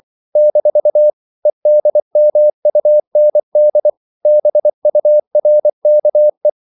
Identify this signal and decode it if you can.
Morse code, heard as a keyed tone, about 24 wpm, 600 Hz. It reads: - EDMUND BURKE
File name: CW / Practice / EnglishQuotations0424